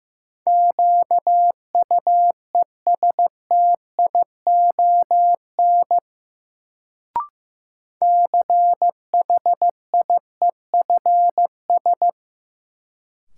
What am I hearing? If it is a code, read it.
Morse code, heard as a keyed tone, about 15 wpm, 700 Hz. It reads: QUESTION CHIEFS
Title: Sets of 5 Words Encore - New Word Drill - 15wpm